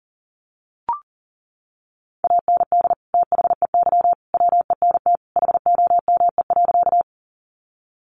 Morse code, heard as a keyed tone, about 40 wpm, 700 Hz. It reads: AND THEY WENT HOME.
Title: Sentences from Top 200 Words 40wpm